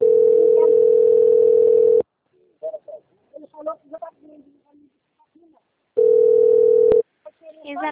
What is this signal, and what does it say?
Ringback tone (North America), 440 + 480 Hz, with speech.
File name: out- - - - - .11157